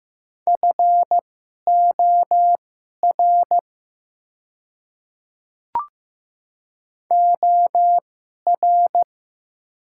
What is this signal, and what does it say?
Morse code, heard as a keyed tone, about 15 wpm, spaced as for 11 wpm, 700 Hz. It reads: FOR OR
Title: 2x Character Spacing - Top 500 Words - 15wpm